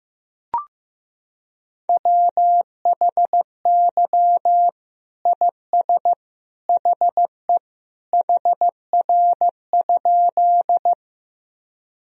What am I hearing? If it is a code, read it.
Morse code, heard as a keyed tone, about 15 wpm, 700 Hz. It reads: WHY IS HE HR?
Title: Sentences from Top 200 Words and Abbreviations 15wpm